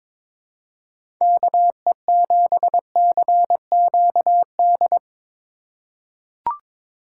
Morse code, heard as a keyed tone, about 22 wpm, 700 Hz. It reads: KE7CQD
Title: USA Call Signs - 6 Characters - 4x Character Spacing - 22wpm